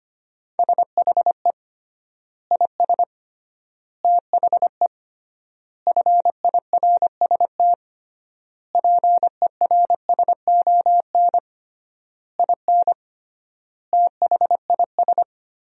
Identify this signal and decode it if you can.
Morse code, heard as a keyed tone, about 25 wpm, 700 Hz. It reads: SHE IS THE FIRST PERSON IN THIS
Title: 3x Word-Spacing - Sentences from Top 400 Words - 25wpm